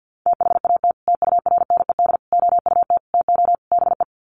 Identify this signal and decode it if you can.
Morse code, heard as a keyed tone, about 50 wpm, 700 Hz. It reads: THAT TURNED OUT TO BE
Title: Sets of 5 Words Encore2 50wpm